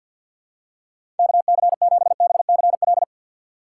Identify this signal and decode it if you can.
Morse code, heard as a keyed tone, about 49 wpm, 700 Hz. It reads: KC7BCL
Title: USA Call Signs - 6x Character Spacing - 50wpm